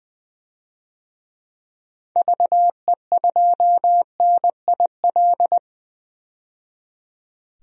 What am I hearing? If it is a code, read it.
Morse code, heard as a keyed tone, about 20 wpm, 700 Hz. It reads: VE2NIL
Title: DXCC-100 Call Signs - 6 Characters - 20wpm